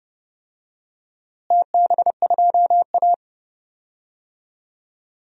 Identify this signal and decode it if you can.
Morse code, heard as a keyed tone, about 31 wpm, 700 Hz. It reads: TB2A
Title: DXCC-100 Call Signs - 4 Characters - Repeated 3x - 30wpm